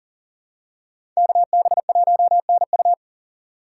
Morse code, heard as a keyed tone, about 40 wpm, 700 Hz. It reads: KB1NU
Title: USA Call Signs - 5 Characters - 2x Character Spacing - 40wpm